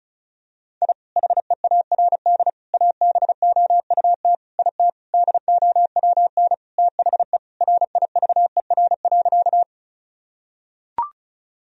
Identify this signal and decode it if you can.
Morse code, heard as a keyed tone, about 34 wpm, 700 Hz. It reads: I HEARD ABOUT IT DOWN THE RIVER.
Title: Sentences from Top 1000 Words - Review Part 1 - 35wpm